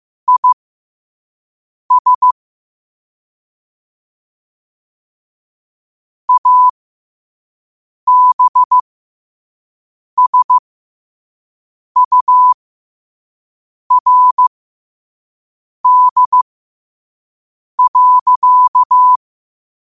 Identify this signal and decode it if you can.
Morse code, heard as a keyed tone, about 15 wpm, spaced as for 5 wpm, 1000 Hz. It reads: IS ABSURD.
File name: qotdfiles / 5wpm